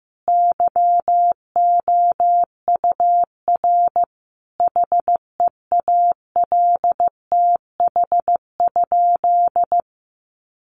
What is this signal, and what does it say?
Morse code, heard as a keyed tone, about 15 wpm, 700 Hz. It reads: YOUR HEALTH?
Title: Sentences from Top 600 Words 15wpm